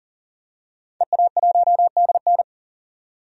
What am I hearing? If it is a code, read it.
Morse code, heard as a keyed tone, about 40 wpm, 700 Hz. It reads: EA1DN
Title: DXCC-100 Call Signs - 5 Characters - 40wpm